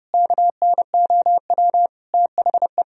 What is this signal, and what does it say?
Morse code, heard as a keyed tone, about 30 wpm, 700 Hz. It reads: KNOW THE
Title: Sentences from Top 1000 Words - Review Part 3 - 30wpm